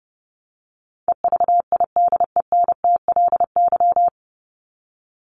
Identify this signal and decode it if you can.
Morse code, heard as a keyed tone, about 30 wpm, 700 Hz. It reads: EVIDENTLY
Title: LY Suffix - Top 100 Words with Base Words 30wpm